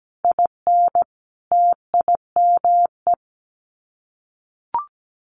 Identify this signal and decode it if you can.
Morse code, heard as a keyed tone, about 17 wpm, 700 Hz. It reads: IN TIME
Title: Sets of 5 Words Encore 17wpm